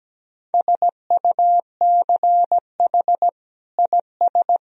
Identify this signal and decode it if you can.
Morse code, heard as a keyed tone, about 17 wpm, 700 Hz. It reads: SUCH IS
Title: Sentences from Top 1000 Words - Review Part 2 - 17wpm